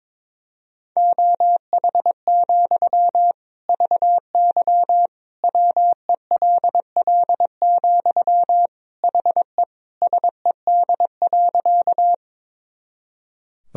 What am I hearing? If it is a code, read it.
Morse code, heard as a keyed tone, about 22 wpm, 700 Hz. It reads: OH, VY WELL, HE SED.
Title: Sentences from Top 500 Words and Abbreviations 22wpm